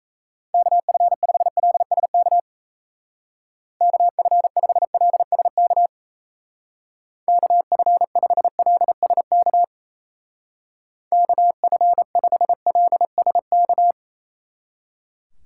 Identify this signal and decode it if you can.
Morse code, heard as a keyed tone, about 33 wpm, 700 Hz. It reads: KF5LSK KF5LSK KF5LSK KF5LSK
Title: Speed Racing - USA 2x3 Call Signs 28wpm